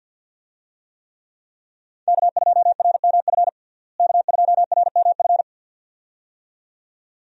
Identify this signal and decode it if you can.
Morse code, heard as a keyed tone, about 47 wpm, 700 Hz. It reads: K2RMF K2RMF
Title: USA Call Signs - 5 Characters - Repeated 2x - 50wpm